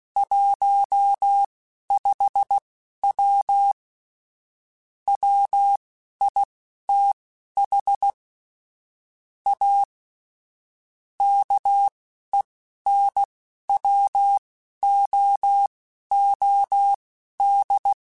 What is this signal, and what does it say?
Morse code, heard as a keyed tone, about 16 wpm, spaced as for 11 wpm, 800 Hz. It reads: 15W WITH A KENWOOD
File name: QSOs / QSOs-mp3s / WD0BGP 1@13